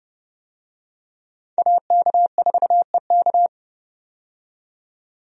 Morse code, heard as a keyed tone, about 30 wpm, 700 Hz. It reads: AK4EK
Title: USA Call Signs - 5 Characters - 30wpm